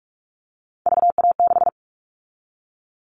Morse code, heard as a keyed tone, about 46 wpm, 700 Hz. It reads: VA6
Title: Callsign Prefixes for Top 50 DXCC - 45wpm